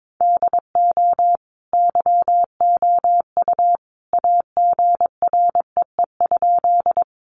Morse code, heard as a keyed tone, about 22 wpm, 700 Hz. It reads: DO YOU AGREE?
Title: Sentences from Top 1100 Words 22wpm